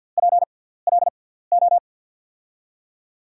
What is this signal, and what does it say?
Morse code, heard as a keyed tone, about 51 wpm, spaced as for 17 wpm, 700 Hz. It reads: PLO
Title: USA Call Signs - 6 Characters - 6x Character Spacing - 50wpm